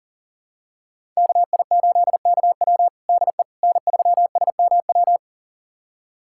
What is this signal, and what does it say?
Morse code, heard as a keyed tone, about 40 wpm, 700 Hz. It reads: KI8KW DE N3SMW
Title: USA Callsign DE Callsign - 5 Characters - 40wpm